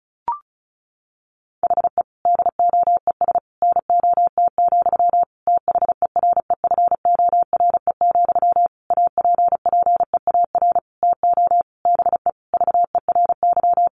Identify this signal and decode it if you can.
Morse code, heard as a keyed tone, about 34 wpm, 700 Hz. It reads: HE DOES NOT, THEREFORE, APPEAR TO BE VERY
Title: Sentences from Top 1200 Words 35wpm